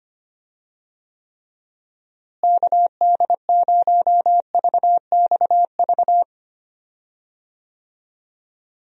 Morse code, heard as a keyed tone, about 25 wpm, 700 Hz. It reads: KD0VXV
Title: USA Call Signs - 6 Characters - Repeated 3x - 25wpm